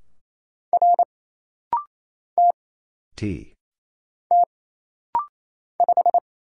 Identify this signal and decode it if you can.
Morse code, heard as a keyed tone, about 28 wpm, 700 Hz. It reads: R T T 5
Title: Single Letter-Number Alt - Rapid-Fire 28wpm